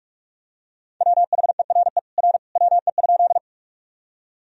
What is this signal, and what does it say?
Morse code, heard as a keyed tone, about 46 wpm, 700 Hz. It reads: WHERE R WE?
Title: Sentences from Top 200 Words and Abbreviations 45wpm